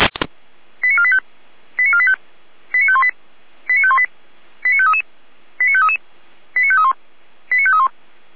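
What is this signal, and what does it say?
Five-tone selective call (ZVEI), heard as quick runs of five tones, tones of about 70 ms. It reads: 89485 89485 89518 89518 89533 89533 89531 89531
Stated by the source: ZVEI five-tone call 89416 received on SDR, Feuerwehr Orlamünde server.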